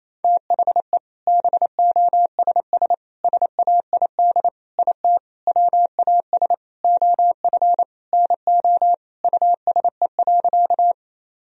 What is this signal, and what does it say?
Morse code, heard as a keyed tone, about 28 wpm, 700 Hz. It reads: THE BOSS SAID IT WAS OF NO USE.
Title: Sentences from Top 1000 Words - Review Part 4 - 28wpm